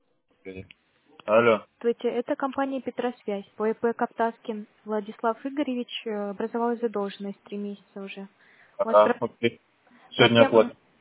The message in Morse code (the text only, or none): none